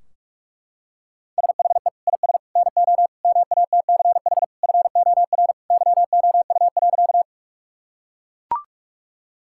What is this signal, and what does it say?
Morse code, heard as a keyed tone, about 41 wpm, 700 Hz. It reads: SHE IS NO MATCH FOR YOU.
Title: Sentences from Top 1600 Words 45wpm